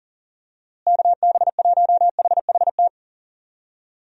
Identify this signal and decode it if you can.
Morse code, heard as a keyed tone, about 40 wpm, 700 Hz. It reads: KB1HHT
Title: USA Call Signs - 6 Characters - Repeated 3x - 40wpm